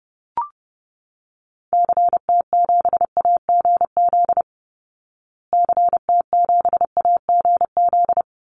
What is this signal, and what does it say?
Morse code, heard as a keyed tone, about 30 wpm, 700 Hz. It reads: CT7AGZ CT7AGZ
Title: Call Signs Repeated 2x - 30wpm